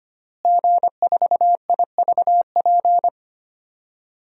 Morse code, heard as a keyed tone, about 25 wpm, 700 Hz. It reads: G4IVP
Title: DXCC-100 Call Signs - 5 Characters - Repeated 3x - 25wpm